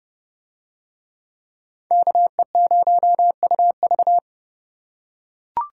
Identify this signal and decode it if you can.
Morse code, heard as a keyed tone, about 30 wpm, 700 Hz. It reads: KE0UV